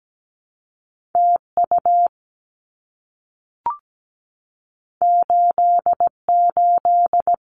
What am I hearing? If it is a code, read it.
Morse code, heard as a keyed tone, about 17 wpm, 700 Hz. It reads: TU 88
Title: Abbreviations Repeated 3x - 17wpm